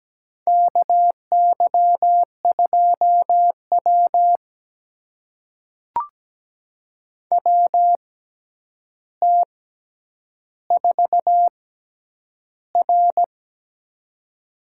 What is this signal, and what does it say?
Morse code, heard as a keyed tone, about 17 wpm, spaced as for 9 wpm, 700 Hz. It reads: KY2W WT4R
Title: USA Call Signs - 4 Characters - 6x Character Spacing - 17wpm